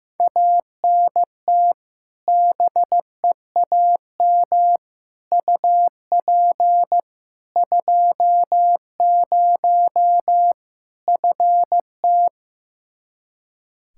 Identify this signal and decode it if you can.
Morse code, heard as a keyed tone, about 15 wpm, 700 Hz. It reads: ANT BEAM UP 20 FT
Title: Practice QSO - Exchange 2 - Info - 15wpm